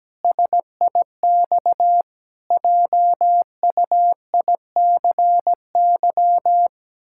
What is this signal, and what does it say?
Morse code, heard as a keyed tone, about 17 wpm, 700 Hz. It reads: SIX JUICY
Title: Pangram Challenge v2 - 17wpm